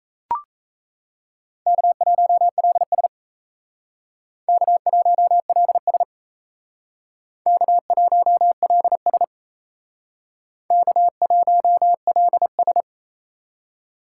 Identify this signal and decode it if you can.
Morse code, heard as a keyed tone, about 36 wpm, 700 Hz. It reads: K1LS K1LS K1LS K1LS